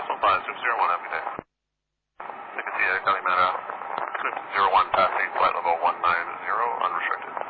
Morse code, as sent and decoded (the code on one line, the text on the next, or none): none